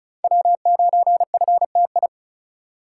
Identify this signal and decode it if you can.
Morse code, heard as a keyed tone, about 34 wpm, 700 Hz. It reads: W9FTI
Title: USA Call Signs - 5 Characters - 4x Character Spacing - 35wpm